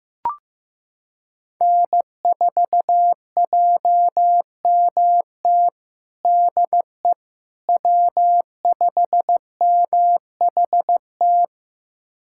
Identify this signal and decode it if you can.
Morse code, heard as a keyed tone, about 16 wpm, 700 Hz. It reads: N4JMT DE W5MHT
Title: USA Callsign DE Callsign - 5 Characters - 15wpm